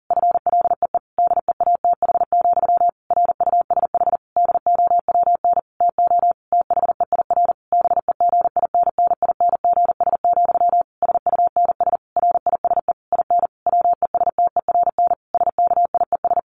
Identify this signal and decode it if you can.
Morse code, heard as a keyed tone, about 40 wpm, 700 Hz. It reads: FLEE DEATH, RUSH DOWN TO THEIR BEGINNINGS, SUNS RISE IN WESTERN SKIES
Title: A Sound Of Thunder 40wpm